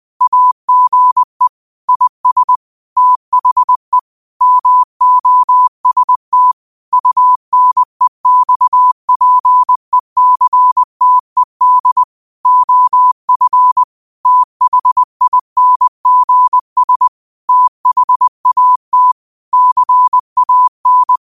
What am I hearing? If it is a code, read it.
Morse code, heard as a keyed tone, about 20 wpm, 1000 Hz. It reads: AGE IS THE MOST UNEXPECTED OF THINGS THAT CAN
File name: qotdfiles / 20wpm